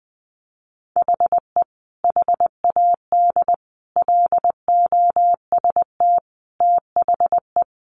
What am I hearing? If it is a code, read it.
Morse code, heard as a keyed tone, about 20 wpm, 700 Hz. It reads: HE HAD LOST THE
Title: Sentences from Top 1500 Words 20wpm